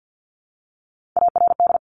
Morse code, heard as a keyed tone, about 50 wpm, 700 Hz. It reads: ARD
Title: Top Three Letter Combinations in Words 50wpm